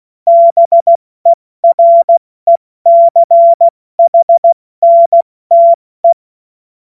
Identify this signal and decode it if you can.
Morse code, heard as a keyed tone, about 16 wpm, 680 Hz. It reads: BERECHNTE
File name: Morsekurs / Witze / Witz 14. Juni 2022 70 F80